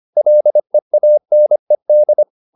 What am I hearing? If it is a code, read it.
Morse code, heard as a keyed tone, about 25 wpm, 600 Hz. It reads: LEANED